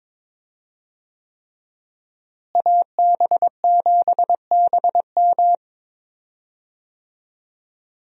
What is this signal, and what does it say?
Morse code, heard as a keyed tone, about 22 wpm, 700 Hz. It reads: AB7BM
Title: USA Call Signs - 5 Characters - Repeated 3x - 22wpm